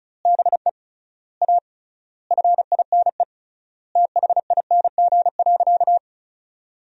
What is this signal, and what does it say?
Morse code, heard as a keyed tone, about 34 wpm, 700 Hz. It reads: BE A FINE THING.
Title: 3x Word-Spacing - Sentences from Top 500 Words - 35wpm